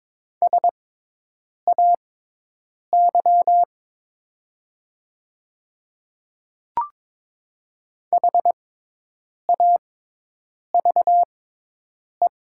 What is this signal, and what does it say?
Morse code, heard as a keyed tone, about 22 wpm, spaced as for 8 wpm, 700 Hz. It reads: SAY HAVE
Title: 6x Character Spacing - Top 300 Words - 22wpm